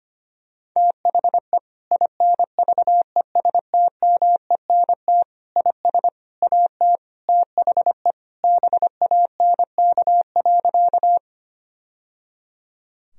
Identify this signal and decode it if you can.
Morse code, heard as a keyed tone, about 25 wpm, 700 Hz. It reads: THE INVESTMENT IS AT THE BANK.
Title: Sentences from Top 1000 Words - Review Part 4 - 25wpm